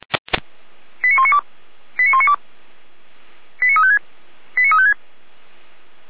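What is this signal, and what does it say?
Five-tone selective call (ZVEI), heard as quick runs of five tones, tones of about 70 ms. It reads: 89192 89192 89356 89356
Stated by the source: Five-tone ZVEI selective call received on an SDR.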